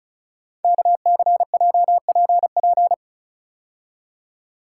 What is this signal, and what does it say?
Morse code, heard as a keyed tone, about 36 wpm, 690 Hz. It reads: KCJPP
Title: Code Groups - 5 Letters - 35wpm